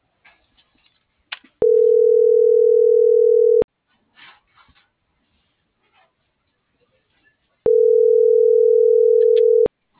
Ringback tone (North America), 440 + 480 Hz.